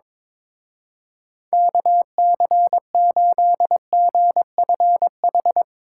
Morse code, heard as a keyed tone, about 24 wpm, 700 Hz. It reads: KC8GFH
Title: Speed Racing - USA 2x3 Call Signs 22wpm